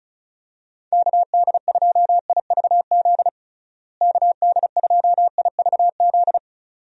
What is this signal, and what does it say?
Morse code, heard as a keyed tone, about 36 wpm, 700 Hz. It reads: KD2IVZ KD2IVZ
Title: USA Call Signs - 6 Characters - Repeated 2x - 35wpm